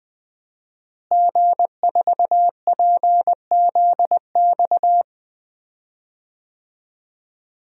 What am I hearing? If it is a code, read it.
Morse code, heard as a keyed tone, about 20 wpm, 700 Hz. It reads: G4PZX
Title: DXCC-100 Call Signs - 5 Characters - Repeated 3x - 20wpm